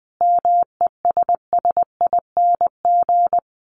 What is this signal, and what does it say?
Morse code, heard as a keyed tone, about 19 wpm, 700 Hz. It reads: MESSING
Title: A Sound Of Thunder 20wpm